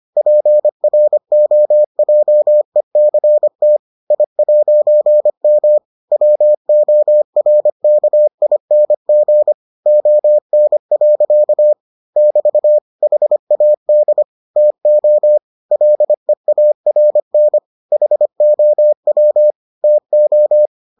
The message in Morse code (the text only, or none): PROJECT I'M WORKING ON. <BT> HAD TO LEARN HOW TO